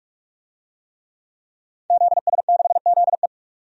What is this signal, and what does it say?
Morse code, heard as a keyed tone, about 48 wpm, 700 Hz. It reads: ZS6ZE
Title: DXCC-100 Call Signs - 5 Characters - 45wpm